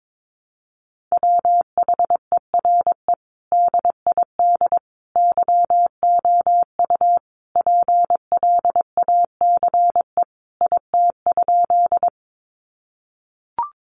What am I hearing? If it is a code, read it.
Morse code, heard as a keyed tone, about 22 wpm, 700 Hz. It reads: WHERE DID YOU PLACE IT?
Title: Sentences from Top 200 Words 22wpm